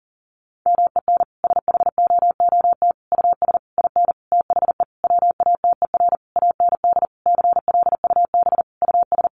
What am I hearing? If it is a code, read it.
Morse code, heard as a keyed tone, about 40 wpm, 700 Hz. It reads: MEN SHOOT US IN THE WATER AND CLUB US
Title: Sentences from Top 1200 Words 40wpm